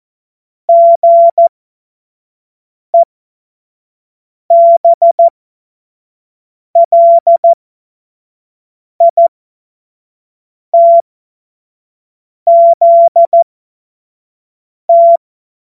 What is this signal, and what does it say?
Morse code, heard as a keyed tone, about 14 wpm, spaced as for 5 wpm, 680 Hz. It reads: GEBLITZT
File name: Witz-05-August-2025-25-F70